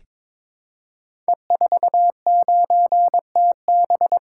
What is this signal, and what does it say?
Morse code, heard as a keyed tone, about 23 wpm, 700 Hz. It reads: E49TB
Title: Code Groups - 5 Letter:Numbers - 22wpm